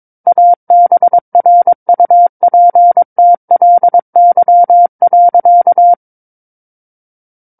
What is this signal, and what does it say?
Morse code, heard as a keyed tone, about 22 wpm, 700 Hz. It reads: ABRUPTLY.
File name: Book One / 13-Chapter 13 - 22 WPM